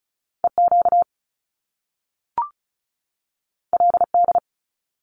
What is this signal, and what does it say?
Morse code, heard as a keyed tone, about 36 wpm, 700 Hz. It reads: EQ LD
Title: Top Two Letter Combinations in Words 35wpm